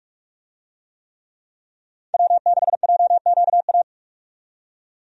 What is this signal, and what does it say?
Morse code, heard as a keyed tone, about 43 wpm, 700 Hz. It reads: W6JQA
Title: USA Call Signs - 5 Characters - 45wpm